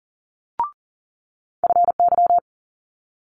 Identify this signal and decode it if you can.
Morse code, heard as a keyed tone, about 37 wpm, 700 Hz. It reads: FY
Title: Multi-letter Phonemes Quick Fire 40wpm